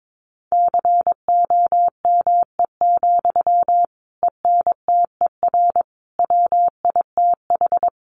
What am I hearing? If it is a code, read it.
Morse code, heard as a keyed tone, about 22 wpm, 700 Hz. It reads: COME, ENTER WITH
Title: Sentences from Top 1300 Words 22wpm